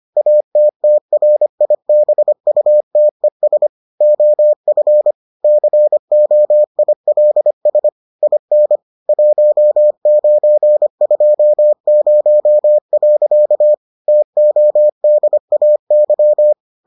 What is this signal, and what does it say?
Morse code, heard as a keyed tone, about 25 wpm, 600 Hz. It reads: ATTRIBUTES OF COILS IN 1920. TODAY